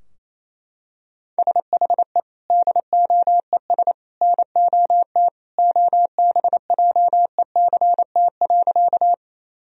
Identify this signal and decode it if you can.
Morse code, heard as a keyed tone, about 27 wpm, 700 Hz. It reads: SHE DOES NOT OBJECT.
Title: Sentences from Top 1000 Words - Review Part 3 - 28wpm